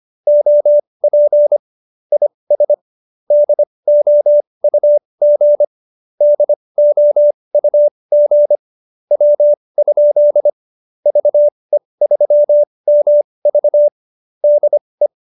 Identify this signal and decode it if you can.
Morse code, heard as a keyed tone, about 25 wpm, spaced as for 20 wpm, 590 Hz. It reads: OP IS DOUG DOUG W? VE3MV DE